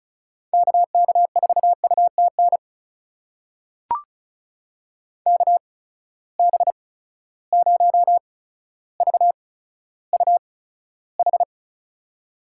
Morse code, heard as a keyed tone, about 36 wpm, spaced as for 15 wpm, 700 Hz. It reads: KK4UTN KB0VUH
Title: USA Call Signs - 6 Characters - 8x Character Spacing - 35wpm